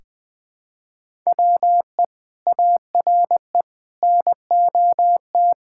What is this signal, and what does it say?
Morse code, heard as a keyed tone, about 20 wpm, 700 Hz. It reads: WE ARE NOT
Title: Build A Phrase - 5-Word Phrases - 20wpm